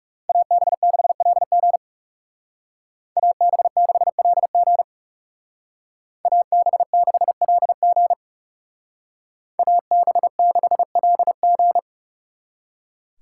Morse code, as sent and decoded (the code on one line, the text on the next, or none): .- -... -.... .-.. --. / .- -... -.... .-.. --. / .- -... -.... .-.. --. / .- -... -.... .-.. --.
AB6LG AB6LG AB6LG AB6LG